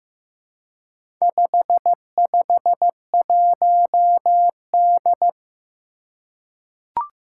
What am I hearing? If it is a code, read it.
Morse code, heard as a keyed tone, about 15 wpm, 700 Hz. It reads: 551D
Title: Four Challenging Letter-Number Combinations 15wpm